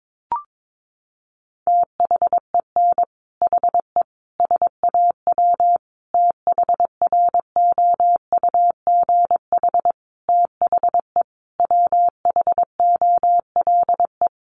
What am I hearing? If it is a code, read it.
Morse code, heard as a keyed tone, about 22 wpm, 700 Hz. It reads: THEN HE SAW THROUGH THE WHOLE
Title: Sentences from Top 400 Words 22wpm